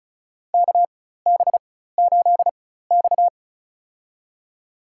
Morse code, heard as a keyed tone, about 36 wpm, spaced as for 16 wpm, 700 Hz. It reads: KB8X
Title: USA Call Signs - 4x Character Spacing - 35wpm